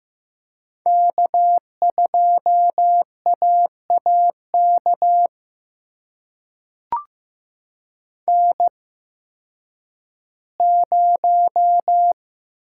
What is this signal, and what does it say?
Morse code, heard as a keyed tone, about 15 wpm, spaced as for 10 wpm, 700 Hz. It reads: K2AAK N0
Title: USA Call Signs - 5 Characters - 8x Character Spacing - 15wpm